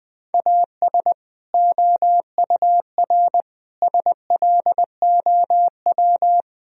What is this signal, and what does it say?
Morse code, heard as a keyed tone, about 21 wpm, 700 Hz. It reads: AS OUR SLOW
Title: Sentences from Top 1000 Words - Review Part 2 - 20wpm